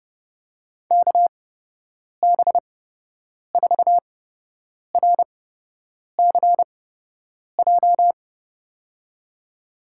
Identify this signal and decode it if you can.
Morse code, heard as a keyed tone, about 30 wpm, spaced as for 8 wpm, 700 Hz. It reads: KB4RCJ
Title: USA Call Signs - 6 Characters - 8x Character Spacing - 30wpm